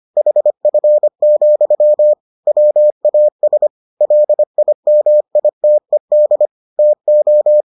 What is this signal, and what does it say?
Morse code, heard as a keyed tone, about 25 wpm, 600 Hz. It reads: HF, WAS LIMITED TO